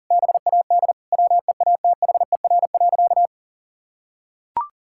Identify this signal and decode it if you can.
Morse code, heard as a keyed tone, about 40 wpm, 700 Hz. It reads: BAD WEATHER.
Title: Sentences from Top 1000 Words - Review Part 6 - 40wpm